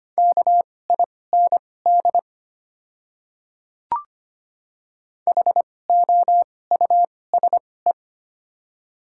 Morse code, heard as a keyed tone, about 25 wpm, spaced as for 18 wpm, 700 Hz. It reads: KIND HOUSE
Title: 2x Character Spacing - Top 500 Words - 25wpm